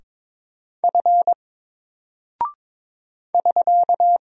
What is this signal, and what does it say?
Morse code, heard as a keyed tone, about 22 wpm, 700 Hz. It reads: F <SK>
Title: Single Character (Letter:Number:Punctuation:Prosign) 22wpm